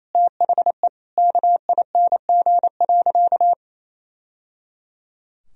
Morse code, heard as a keyed tone, about 28 wpm, 700 Hz. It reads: THE KING.
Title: Sentences from Top 1000 Words - Review Part 4 - 28wpm